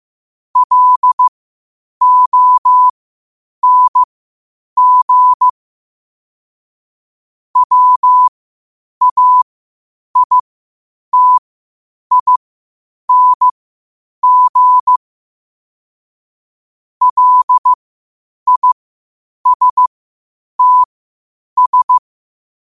Morse code, heard as a keyed tone, about 15 wpm, spaced as for 8 wpm, 1000 Hz. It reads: LONG WAITING LISTS
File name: qotdfiles / 8wpm